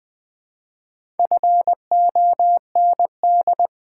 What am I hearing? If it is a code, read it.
Morse code, heard as a keyed tone, about 20 wpm, 700 Hz. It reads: FOND
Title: Build A Word - 1 Letter At A Time - 3 to 5-Letter Words - 20wpm